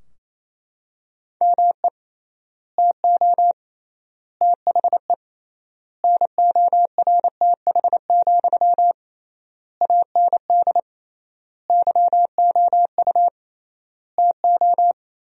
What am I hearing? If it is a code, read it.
Morse code, heard as a keyed tone, about 28 wpm, 700 Hz. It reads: ME TO THE NORTH, AND YOU TO
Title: 3x Word-Spacing - Sentences from Top 400 Words - 28wpm